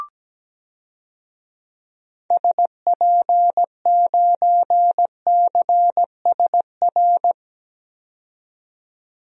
Morse code, heard as a keyed tone, about 17 wpm, 700 Hz. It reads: SP9CSR
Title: DXCC-100 Call Signs - 6 Characters - 17wpm